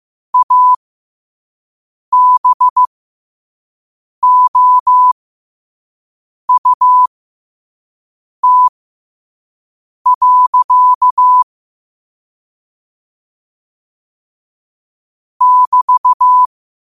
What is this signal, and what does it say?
Morse code, heard as a keyed tone, about 15 wpm, spaced as for 5 wpm, 1000 Hz. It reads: ABOUT. <BT>
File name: qotdfiles / 5wpm